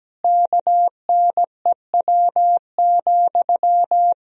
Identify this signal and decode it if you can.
Morse code, heard as a keyed tone, about 17 wpm, 700 Hz. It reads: KNEW,
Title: Sentences from Top 1000 Words - Review Part 4 - 17wpm